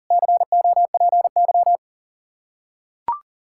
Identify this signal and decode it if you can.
Morse code, heard as a keyed tone, about 40 wpm, 700 Hz. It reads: COPY
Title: Practice QSO - Exchange 3 - More Info - 40wpm